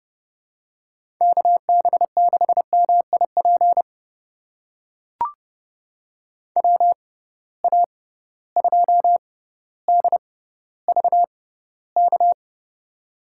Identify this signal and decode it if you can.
Morse code, heard as a keyed tone, about 30 wpm, spaced as for 15 wpm, 700 Hz. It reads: KB6MIP WA2DVK
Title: USA Call Signs - 6 Characters - 6x Character Spacing - 30wpm